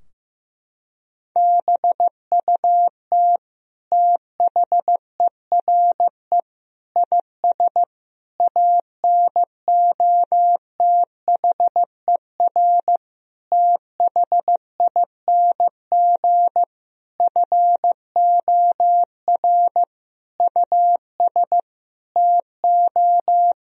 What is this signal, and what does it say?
Morse code, heard as a keyed tone, about 15 wpm, 700 Hz. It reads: BUT THERE IS ANOTHER THING FOR US TO
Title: Sentences from Top 1000 Words - Review Part 4 - 15wpm